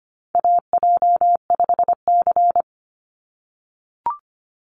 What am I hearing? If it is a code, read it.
Morse code, heard as a keyed tone, about 25 wpm, 700 Hz. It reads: AJ5C
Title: NA SOTA Chaser Callsigns 25wpm